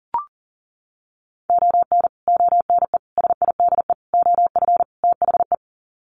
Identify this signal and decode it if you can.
Morse code, heard as a keyed tone, about 40 wpm, 700 Hz. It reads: ON ONE SIDE OF THE
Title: Sets of 5 Words 40wpm